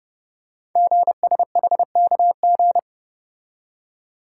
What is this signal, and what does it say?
Morse code, heard as a keyed tone, about 30 wpm, 700 Hz. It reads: GSHKG
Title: Code Groups - 5 Letters - 30wpm